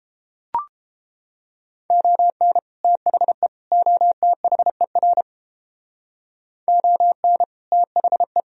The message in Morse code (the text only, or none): ON THE OTHER ON THE